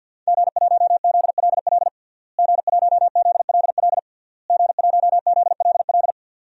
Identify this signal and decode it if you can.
Morse code, heard as a keyed tone, about 50 wpm, 700 Hz. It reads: G1ZLL G1ZLL G1ZLL
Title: DXCC-100 Call Signs - 5 Characters - Repeated 3x - 50wpm